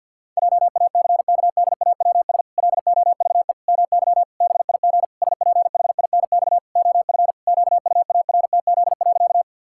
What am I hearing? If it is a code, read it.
Morse code, heard as a keyed tone, about 52 wpm, 700 Hz. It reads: JACKDAWS LOVE MY BIG SPHINX OF QUARTZ.
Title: Pangram Challenge v2 - 50wpm